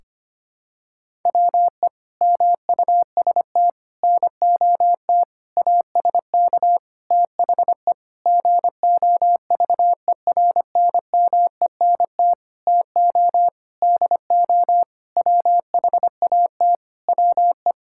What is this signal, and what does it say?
Morse code, heard as a keyed tone, about 25 wpm, 700 Hz. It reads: WE MUST NOT ASK THE GOVERNMENT TO DO WHAT WE
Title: Sentences from Top 1600 Words 25wpm